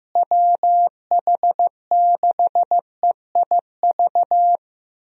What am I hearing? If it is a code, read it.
Morse code, heard as a keyed tone, about 15 wpm, 700 Hz. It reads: WH6EIV
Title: USA Call Signs - 6 Characters - Repeated 3x - 15wpm